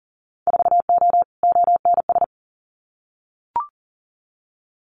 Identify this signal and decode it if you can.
Morse code, heard as a keyed tone, about 37 wpm, 700 Hz. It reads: 4O ONS
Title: Winter Field Day Exchanges 40wpm